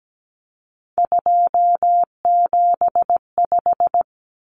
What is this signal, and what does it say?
Morse code, heard as a keyed tone, about 17 wpm, 700 Hz. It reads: 275
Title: Common Cut Numbers - 3-Digit 17wpm